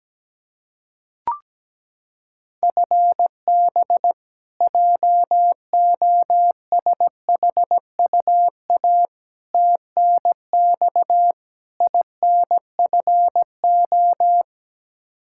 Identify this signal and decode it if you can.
Morse code, heard as a keyed tone, about 17 wpm, 700 Hz. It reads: FB JOSHUA TNX INFO
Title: K1USN SST QSO Practice 17wpm